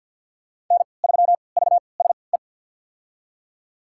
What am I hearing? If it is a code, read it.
Morse code, heard as a keyed tone, about 46 wpm, spaced as for 27 wpm, 700 Hz. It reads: N3VSE
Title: USA Call Signs - 5 Characters - 3x Character Spacing - 50wpm